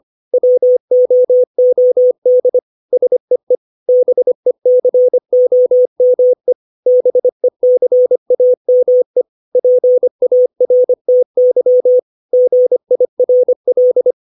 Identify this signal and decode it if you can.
Morse code, heard as a keyed tone, about 25 wpm, 500 Hz. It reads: WOOD SEE BECOME BECAME PARTY GIRL
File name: 632 words / 632 Words 25WPM